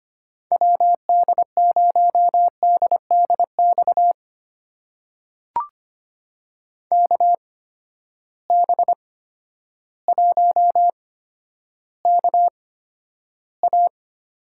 Morse code, heard as a keyed tone, about 25 wpm, spaced as for 11 wpm, 700 Hz. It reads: WD0DDX KB1KA
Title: USA Call Signs - 6 Characters - 8x Character Spacing - 25wpm